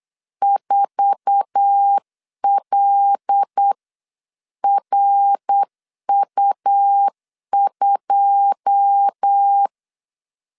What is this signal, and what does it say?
Morse code, heard as a keyed tone, about 8 wpm, 800 Hz. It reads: 4LRU2
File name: CW / DET / 1 9 1500C S A 8wpm 5637